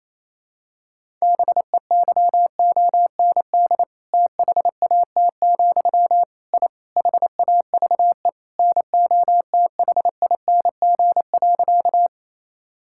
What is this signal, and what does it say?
Morse code, heard as a keyed tone, about 28 wpm, 700 Hz. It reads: BEYOND THAT, I HAVE NOTHING.
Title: Sentences from Top 1000 Words - Review Part 6 - 28wpm